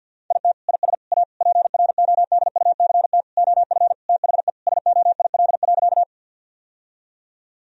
Morse code, heard as a keyed tone, about 50 wpm, 700 Hz. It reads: IT IS A PRODUCT OF THE SOIL.
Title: Sentences from Top 1000 Words - Review Part 4 - 50wpm